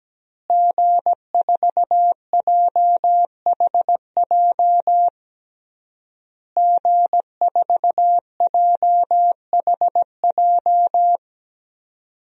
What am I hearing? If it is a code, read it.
Morse code, heard as a keyed tone, about 17 wpm, 700 Hz. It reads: G4JHJ G4JHJ
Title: DXCC-100 Call Signs - 5 Characters - Repeated 3x - 17wpm